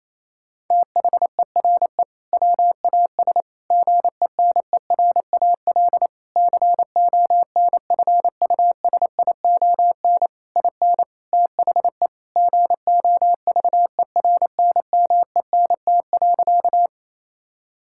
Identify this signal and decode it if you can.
Morse code, heard as a keyed tone, about 28 wpm, 700 Hz. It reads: THERE WAS GENERAL CONFUSION IN THE GOVERNMENT.
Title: Sentences from Top 1600 Words 28wpm